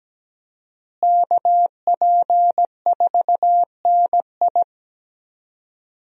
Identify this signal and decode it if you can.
Morse code, heard as a keyed tone, about 17 wpm, 700 Hz. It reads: KP4NI
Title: USA Call Signs - 5 Characters - Repeated 3x - 17wpm